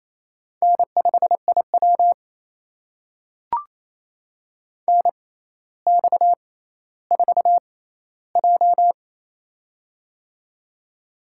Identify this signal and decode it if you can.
Morse code, heard as a keyed tone, about 28 wpm, spaced as for 14 wpm, 700 Hz. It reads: N5IW NX4J
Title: USA Call Signs - 4 Characters - 6x Character Spacing - 28wpm